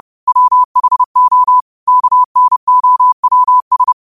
Morse code, heard as a keyed tone, about 30 wpm, 1000 Hz. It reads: WHO KNOWS
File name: qotdfiles / 30wpm